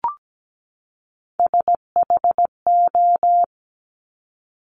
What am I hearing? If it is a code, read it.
Morse code, heard as a keyed tone, about 17 wpm, 700 Hz. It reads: SHO
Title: Top Three Letter Combinations in Words 17wpm